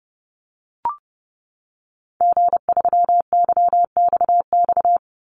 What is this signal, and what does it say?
Morse code, heard as a keyed tone, about 30 wpm, 700 Hz. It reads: G3YXX
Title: Call Signs - 30wpm